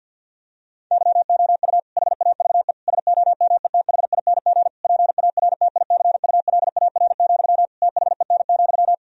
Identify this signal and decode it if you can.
Morse code, heard as a keyed tone, about 50 wpm, 700 Hz. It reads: YOU HAVE SOMETHING PARTICULAR, THEN,
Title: Sentences from Top 1000 Words - Review Part 4 - 50wpm